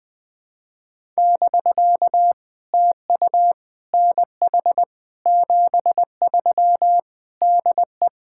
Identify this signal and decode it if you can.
Morse code, heard as a keyed tone, about 20 wpm, 700 Hz. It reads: <BK> TU NH 73 DE
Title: POTA QSO Practice - 20wpm